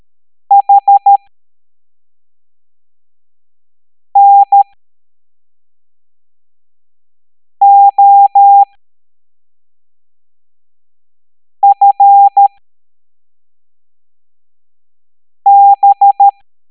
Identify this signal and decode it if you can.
Morse code, heard as a keyed tone, about 13 wpm, spaced as for 3 wpm, 800 Hz. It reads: HNOFB
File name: hamdata / L07-EX03-HQP